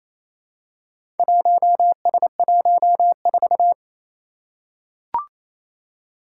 Morse code, heard as a keyed tone, about 28 wpm, 700 Hz. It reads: 1S14
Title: Four Challenging Letter-Number Combinations 28wpm